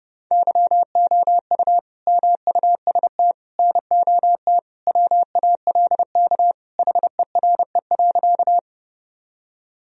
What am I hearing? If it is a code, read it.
Morse code, heard as a keyed tone, about 32 wpm, 700 Hz. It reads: YOU MUST NOT WALK HERE.
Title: Sentences from Top 1000 Words - Review Part 1 - 30wpm